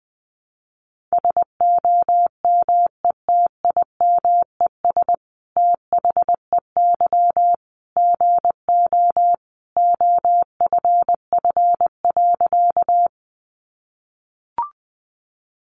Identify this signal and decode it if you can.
Morse code, heard as a keyed tone, about 20 wpm, 700 Hz. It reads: SOMETIMES THEY GO OFF.
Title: Sentences from Top 600 Words 20wpm